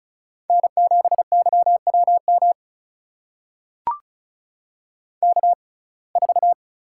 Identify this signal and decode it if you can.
Morse code, heard as a keyed tone, about 34 wpm, spaced as for 25 wpm, 700 Hz. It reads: N7YWM K4
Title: USA Call Signs - 5 Characters - 6x Character Spacing - 35wpm